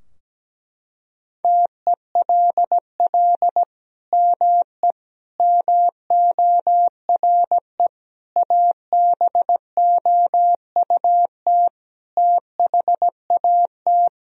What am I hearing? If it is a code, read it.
Morse code, heard as a keyed tone, about 17 wpm, 700 Hz. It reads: TELL ME MORE ABOUT THAT
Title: Sentences from Top 1000 Words - Review Part 3 - 17wpm